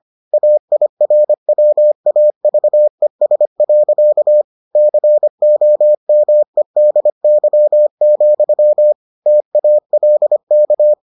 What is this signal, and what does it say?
Morse code, heard as a keyed tone, about 25 wpm, 600 Hz. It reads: AIRWAVES. COMEDY, TALK